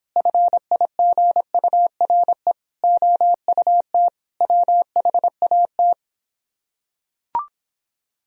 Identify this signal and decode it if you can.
Morse code, heard as a keyed tone, about 26 wpm, 700 Hz. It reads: FIGURE OUT WHAT